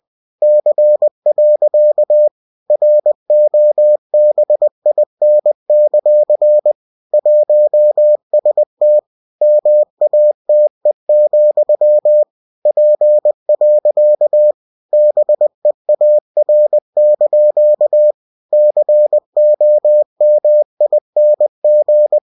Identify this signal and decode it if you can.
Morse code, heard as a keyed tone, about 20 wpm, 600 Hz. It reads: C. ROBIN; 1ST MATE, P. BEAR) COMING
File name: cw-books / winnie-the-pooh / winnie-20wpm-page-0101